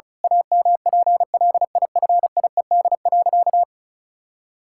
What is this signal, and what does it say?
Morse code, heard as a keyed tone, about 36 wpm, 700 Hz. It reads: AMPLIFIED.
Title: Pangram Challenge v2 - 35wpm